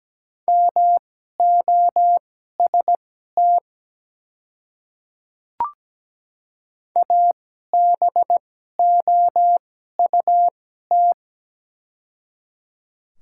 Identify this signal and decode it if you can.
Morse code, heard as a keyed tone, about 17 wpm, spaced as for 12 wpm, 700 Hz. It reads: MOST ABOUT
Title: 2x Character Spacing - Top 100 Words - 17wpm